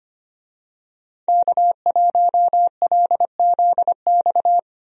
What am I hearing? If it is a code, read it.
Morse code, heard as a keyed tone, about 25 wpm, 700 Hz. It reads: K1LZX
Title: USA Call Signs - 5 Characters - Repeated 2x - 25wpm